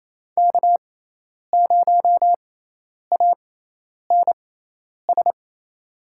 Morse code, heard as a keyed tone, about 28 wpm, spaced as for 10 wpm, 700 Hz. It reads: K0ANS
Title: USA Call Signs - 5 Characters - 6x Character Spacing - 28wpm